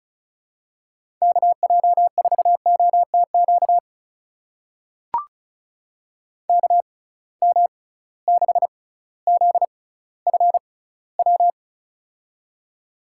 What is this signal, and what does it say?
Morse code, heard as a keyed tone, about 36 wpm, spaced as for 18 wpm, 700 Hz. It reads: KJ4OTQ KM6ZFW
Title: USA Call Signs - 6 Characters - 6x Character Spacing - 35wpm